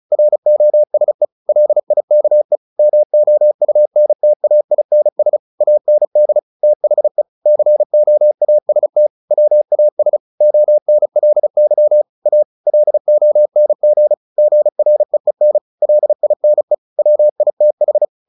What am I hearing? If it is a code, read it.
Morse code, heard as a keyed tone, about 34 wpm, 600 Hz. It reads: ROSE LIKE MOUNTAINS AND THE COAST WAS ONLY A LONG GREEN LINE WITH
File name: the old man and the sea / the old man and the sea 1 35wpm